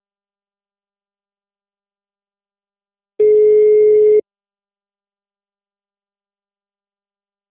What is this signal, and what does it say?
Ringback tone (Europe, 425 Hz).
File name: out- -2033- - - .56393